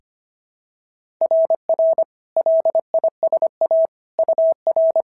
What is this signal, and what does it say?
Morse code, heard as a keyed tone, about 23 wpm, 650 Hz. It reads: RR LISA UR
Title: SOTA QSO Practice 25wpm